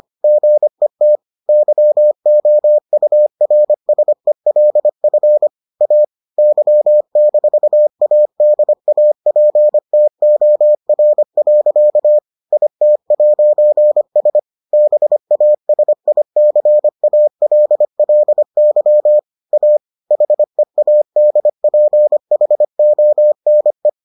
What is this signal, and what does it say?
Morse code, heard as a keyed tone, about 25 wpm, 600 Hz. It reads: GET YOURSELF A Y-ADAPTOR. IT'S BASICALLY A HEADPHONE